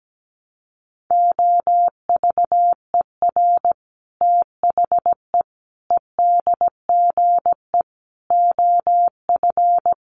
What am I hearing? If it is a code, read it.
Morse code, heard as a keyed tone, about 17 wpm, 700 Hz. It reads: OVER THE EDGE OF
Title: Sets of 5 Words Encore2 17wpm